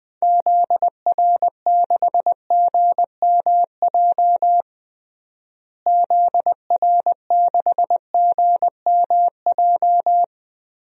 Morse code, heard as a keyed tone, about 21 wpm, 700 Hz. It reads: ZR6GMJ ZR6GMJ
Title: DXCC-100 Call Signs - 6 Characters - Repeated 3x - 20wpm